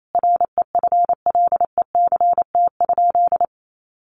Morse code, heard as a keyed tone, about 28 wpm, 700 Hz. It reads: REFLECT?
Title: Sentences from Top 1500 Words 28wpm